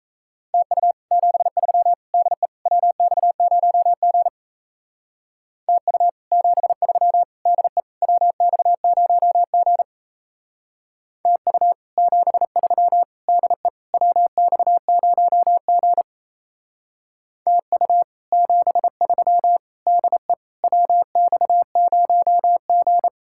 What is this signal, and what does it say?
Morse code, heard as a keyed tone, about 36 wpm, 700 Hz. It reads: TU 73 DE WX0G TU 73 DE WX0G TU 73 DE WX0G TU 73 DE WX0G